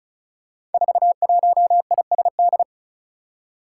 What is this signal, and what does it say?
Morse code, heard as a keyed tone, about 36 wpm, 700 Hz. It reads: 41ISD
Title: Five Challenging Letter-Number Combinations 35wpm